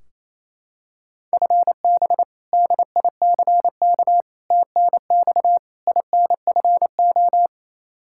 Morse code, heard as a keyed tone, about 28 wpm, 700 Hz. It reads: FB DICK TNX INFO
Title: Practice QSO - Exchange 3 - More Info - 28wpm